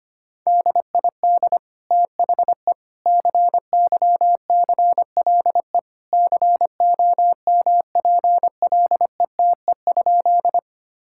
Morse code, heard as a keyed tone, about 25 wpm, 700 Hz. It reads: DID THE CYCLE COMPLETE?
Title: Sentences from Top 1000 Words - Review Part 6 - 25wpm